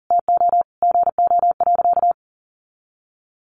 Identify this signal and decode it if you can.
Morse code, heard as a keyed tone, about 40 wpm, 700 Hz. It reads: TO GO.
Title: Sentences from Top 900 Words 40wpm